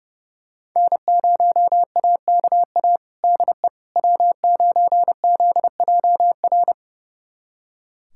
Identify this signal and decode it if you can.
Morse code, heard as a keyed tone, about 31 wpm, 700 Hz. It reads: N0AKA DE W9ZJR
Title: USA Callsign DE Callsign - 5 Characters - 30wpm